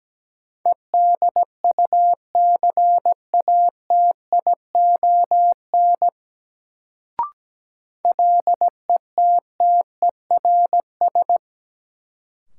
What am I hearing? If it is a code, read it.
Morse code, heard as a keyed tone, about 17 wpm, 700 Hz. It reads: EDUCATION LETTERS
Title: Sentences from Top 1000 Words - Review Part 6 - New Word Drill - 17wpm